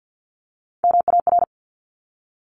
Morse code, heard as a keyed tone, about 50 wpm, 700 Hz. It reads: MAR
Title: ARRL Sections 50wpm